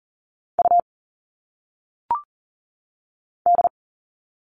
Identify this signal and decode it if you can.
Morse code, heard as a keyed tone, about 40 wpm, 700 Hz. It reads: U D
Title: All Cut Numbers - 1-Digit 40wpm